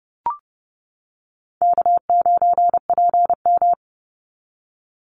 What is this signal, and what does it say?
Morse code, heard as a keyed tone, about 28 wpm, 700 Hz. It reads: K9PM
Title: NA SOTA Activator Callsigns 30wpm